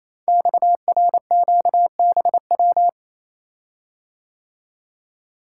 Morse code, heard as a keyed tone, about 27 wpm, 700 Hz. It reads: XRQBW
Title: Code Groups - 5 Letters - 28wpm